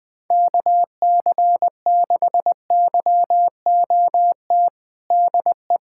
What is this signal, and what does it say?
Morse code, heard as a keyed tone, about 20 wpm, 700 Hz. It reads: KC6YOT DE
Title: USA Callsign DE Callsign 20wpm